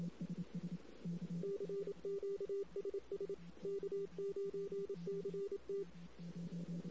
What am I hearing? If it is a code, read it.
Morse code, heard as a keyed tone, about 27 wpm, 400 Hz. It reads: CQSS K9CT
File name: CW / 2008-ss-cw-k5tr-op-n5rz -16:00